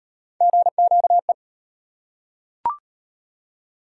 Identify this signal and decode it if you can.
Morse code, heard as a keyed tone, about 38 wpm, 700 Hz. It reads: GQE